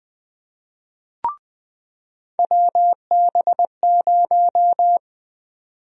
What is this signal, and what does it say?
Morse code, heard as a keyed tone, about 20 wpm, 700 Hz. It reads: WB0
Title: USA Call Sign Prefixes - 20wpm